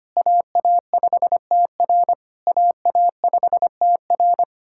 Morse code, heard as a keyed tone, about 25 wpm, 700 Hz. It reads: AA5TR AA5TR
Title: SOTA QSO Practice 25wpm